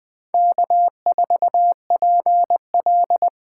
Morse code, heard as a keyed tone, about 20 wpm, 700 Hz. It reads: K4PL
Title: Speed Racing - USA Call Signs 17wpm